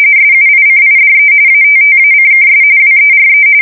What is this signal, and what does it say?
RTTY, about 45.45 baud. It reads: SUICIDE ATTACKS WERE
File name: fsk170 / battleshipnj